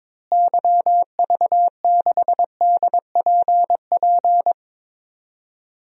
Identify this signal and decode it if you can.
Morse code, heard as a keyed tone, about 22 wpm, 700 Hz. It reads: YV6DPP
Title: DXCC-100 Call Signs - 6 Characters - Repeated 2x - 22wpm